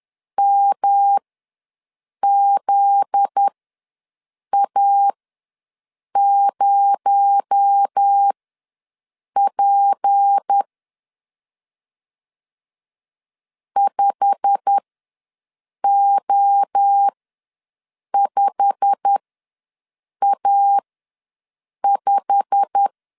Morse code, heard as a keyed tone, about 10 wpm, spaced as for 6 wpm, 800 Hz. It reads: MZA0P 5O5A5
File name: LEZIONI / 8 YJ45 / 8 YJ45 ws 10wpm 3843